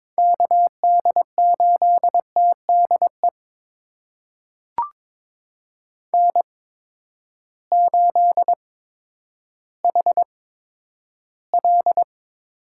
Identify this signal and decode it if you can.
Morse code, heard as a keyed tone, about 22 wpm, spaced as for 11 wpm, 700 Hz. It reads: KD8TDE N8HL
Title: USA Call Signs - 8x Character Spacing - 22wpm